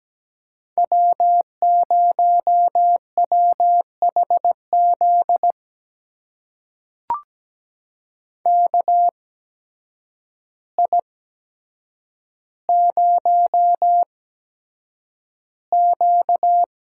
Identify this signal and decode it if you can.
Morse code, heard as a keyed tone, about 17 wpm, spaced as for 8 wpm, 700 Hz. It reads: W0WHZ KI0Q
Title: USA Call Signs - 5 Characters - 8x Character Spacing - 17wpm